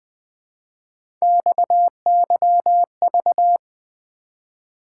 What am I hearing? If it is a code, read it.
Morse code, heard as a keyed tone, about 20 wpm, 700 Hz. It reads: XYV
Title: Three Letter-Number Combinations - 20wpm